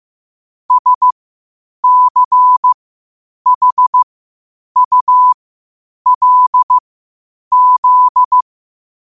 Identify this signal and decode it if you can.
Morse code, heard as a keyed tone, about 15 wpm, spaced as for 8 wpm, 1000 Hz. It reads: SCHULZ
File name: qotdfiles / 8wpm